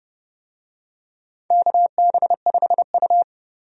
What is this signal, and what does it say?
Morse code, heard as a keyed tone, about 30 wpm, 700 Hz. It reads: KB5U
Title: USA Call Signs - 4 Characters - Repeated 3x - 30wpm